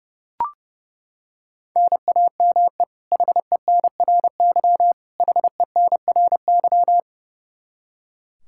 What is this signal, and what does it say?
Morse code, heard as a keyed tone, about 30 wpm, 700 Hz. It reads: NAME HENRY HENRY
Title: Practice QSO - Exchange 1 - Beginning - 30wpm